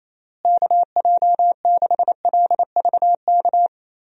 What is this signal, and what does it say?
Morse code, heard as a keyed tone, about 28 wpm, 700 Hz. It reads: KJ6LVK
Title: USA Call Signs - 6 Characters - Repeated 3x - 28wpm